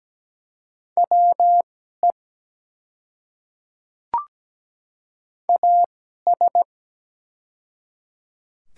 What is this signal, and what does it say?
Morse code, heard as a keyed tone, about 17 wpm, spaced as for 12 wpm, 700 Hz. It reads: WE AS
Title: 2x Character Spacing - Top 100 Words - 17wpm